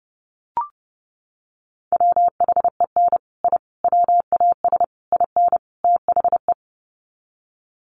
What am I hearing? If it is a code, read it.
Morse code, heard as a keyed tone, about 30 wpm, 700 Hz. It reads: WHEN I WAS IN THE
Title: Sets of 5 Words Encore2 30wpm